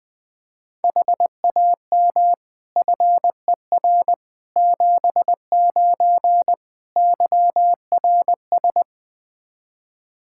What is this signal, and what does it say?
Morse code, heard as a keyed tone, about 20 wpm, 700 Hz. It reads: HAM FER 79 YRS
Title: Practice QSO - Exchange 3 - More Info - 20wpm